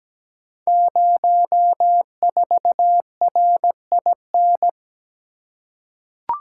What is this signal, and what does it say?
Morse code, heard as a keyed tone, about 17 wpm, 700 Hz. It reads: 04RIN
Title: Code Groups - 5 Letter:Numbers - 17wpm